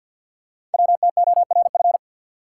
Morse code, heard as a keyed tone, about 46 wpm, 690 Hz. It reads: WTORF
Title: Code Groups - 5 Letter:Numbers - 50wpm